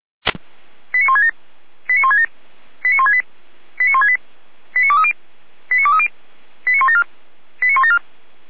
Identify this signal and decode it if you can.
Five-tone selective call (ZVEI), heard as quick runs of five tones, tones of about 70 ms. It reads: 89167 89167 89168 89168 89230 89230 89174 89174
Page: SDR/ZVEI